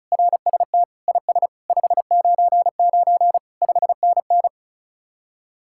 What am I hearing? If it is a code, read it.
Morse code, heard as a keyed tone, about 43 wpm, 700 Hz. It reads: RST IS 599 5NN
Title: Speed-Racing Common QSO Phrases Encore - 35wpm